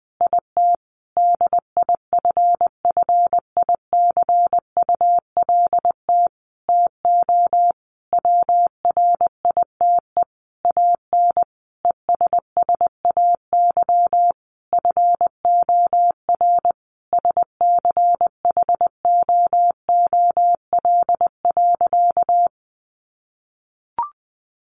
Morse code, heard as a keyed tone, about 19 wpm, 700 Hz. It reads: IT DIFFICULT TO WRITE AN ESSAY FOR SCHOOL.
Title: Sentences from Top 1500 Words 20wpm